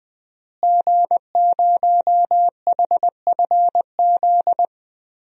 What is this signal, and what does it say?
Morse code, heard as a keyed tone, about 20 wpm, 700 Hz. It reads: G0HFZ
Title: DXCC-100 Call Signs - 5 Characters - Repeated 3x - 20wpm